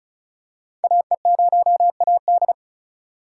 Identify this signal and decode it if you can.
Morse code, heard as a keyed tone, about 36 wpm, 690 Hz. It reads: AE0AD
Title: Code Groups - 5 Letter:Numbers - 35wpm